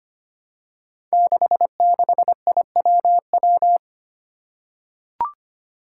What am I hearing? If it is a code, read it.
Morse code, heard as a keyed tone, about 25 wpm, 700 Hz. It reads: 66IWW
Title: Five Challenging Letter-Number Combinations 25wpm